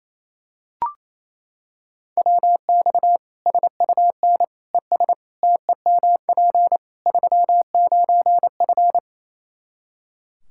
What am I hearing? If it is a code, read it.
Morse code, heard as a keyed tone, about 27 wpm, 700 Hz. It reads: WX SUN ES TEMP 39F
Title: Practice QSO - Exchange 2 - Info - 28wpm